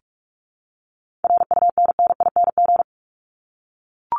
Morse code, heard as a keyed tone, about 44 wpm, 700 Hz. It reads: RUNNING
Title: ING Suffix - Top 100 Words with Base Words 45wpm